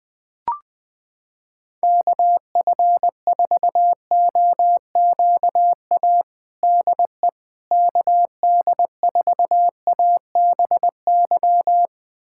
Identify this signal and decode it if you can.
Morse code, heard as a keyed tone, about 19 wpm, 700 Hz. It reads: KF4OQA DE KD4ABY
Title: USA Callsign DE Callsign 20wpm